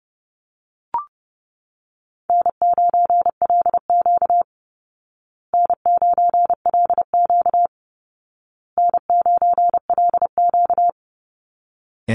Morse code, heard as a keyed tone, about 30 wpm, 700 Hz. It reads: N9LQ N9LQ N9LQ
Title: Easy Call Signs Repeated 3x - 30wpm